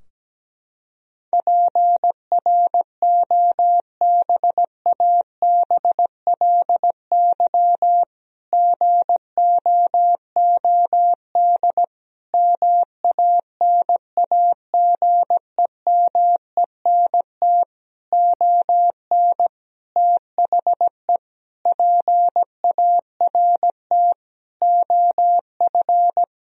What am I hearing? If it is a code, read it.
Morse code, heard as a keyed tone, about 17 wpm, 700 Hz. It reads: PROBABLY GOOD MANAGEMENT ON THE PART OF
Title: Sentences from Top 1000 Words - Review Part 4 - 17wpm